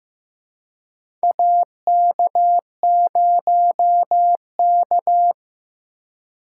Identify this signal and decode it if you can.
Morse code, heard as a keyed tone, about 15 wpm, 700 Hz. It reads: AK0K
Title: USA Call Signs - 4 Characters - 3x Character Spacing - 15wpm